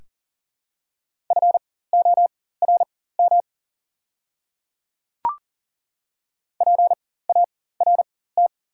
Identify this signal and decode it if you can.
Morse code, heard as a keyed tone, about 40 wpm, spaced as for 19 wpm, 700 Hz. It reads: FORM PART
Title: 4x Character Spacing - Top 300 Words - 40wpm